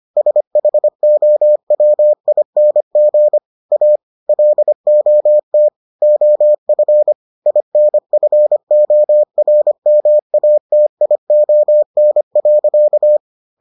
Morse code, heard as a keyed tone, about 25 wpm, 600 Hz. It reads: SHOWING A LOT OF INFORMATION.